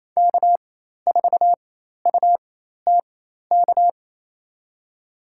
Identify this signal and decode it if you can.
Morse code, heard as a keyed tone, about 28 wpm, spaced as for 13 wpm, 690 Hz. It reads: K4UTK
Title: USA Call Signs - 5 Characters - 4x Character Spacing - 28wpm